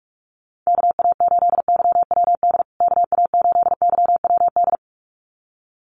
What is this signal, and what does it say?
Morse code, heard as a keyed tone, about 44 wpm, 700 Hz. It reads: KA8YWD KA8YWD
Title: USA Call Signs Repeated 2x 45wpm